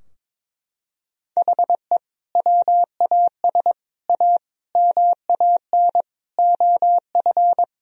Morse code, heard as a keyed tone, about 22 wpm, 700 Hz. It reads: HE WAS A MAN OF
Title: Sentences from Top 1000 Words - Review Part 3 - 22wpm